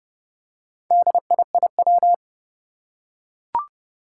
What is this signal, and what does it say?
Morse code, heard as a keyed tone, about 30 wpm, 700 Hz. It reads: DIIW
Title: Four Challenging Letter-Number Combinations 30wpm